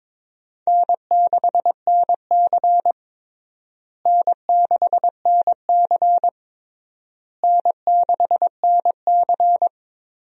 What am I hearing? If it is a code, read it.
Morse code, heard as a keyed tone, about 22 wpm, 700 Hz. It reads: N6NC N6NC N6NC
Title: USA Call Signs - 4 Characters - Repeated 3x - 22wpm